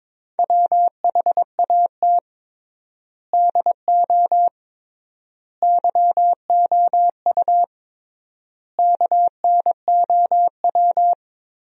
Morse code, heard as a keyed tone, about 22 wpm, 700 Hz. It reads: WHAT DO YOU KNOW
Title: 3x Word-Spacing - Sentences from Top 100 Words - 22wpm